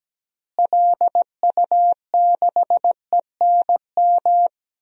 Morse code, heard as a keyed tone, about 17 wpm, 700 Hz. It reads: LU6ENM
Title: DXCC-100 Call Signs - 6 Characters - Repeated 3x - 17wpm